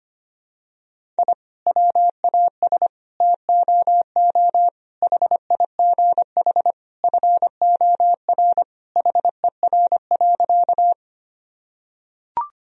Morse code, heard as a keyed tone, about 25 wpm, 700 Hz. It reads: I WAS TOO HIGH FOR HER.
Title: Sentences from Top 1000 Words - Review Part 1 - 25wpm